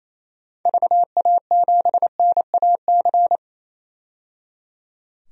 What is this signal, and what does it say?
Morse code, heard as a keyed tone, about 28 wpm, 700 Hz. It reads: VA7NAC
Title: DXCC-100 Call Signs - 6 Characters - 28wpm